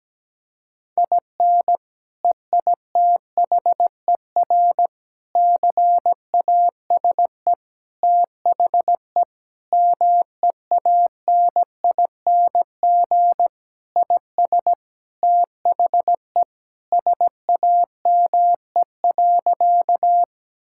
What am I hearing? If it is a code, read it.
Morse code, heard as a keyed tone, about 17 wpm, 700 Hz. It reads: IN EITHER CASE THE MEANING IS THE SAME.
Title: Sentences from Top 1000 Words - Review Part 6 - 17wpm